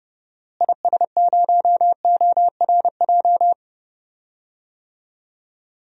Morse code, heard as a keyed tone, about 31 wpm, 700 Hz. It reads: IS0ORJ
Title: DXCC-100 Call Signs - 6 Characters - Repeated 2x - 30wpm